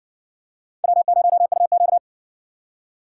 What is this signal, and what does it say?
Morse code, heard as a keyed tone, about 57 wpm, 700 Hz. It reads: W0UQ